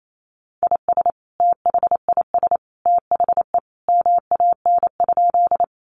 Morse code, heard as a keyed tone, about 28 wpm, 700 Hz. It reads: IS THIS THE MAN?
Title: Sentences from Top 200 Words 28wpm